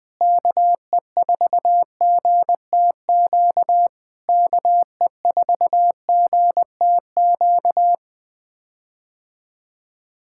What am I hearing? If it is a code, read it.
Morse code, heard as a keyed tone, about 20 wpm, 700 Hz. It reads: KE4GTQ KE4GTQ
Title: POTA QSO Practice - 20wpm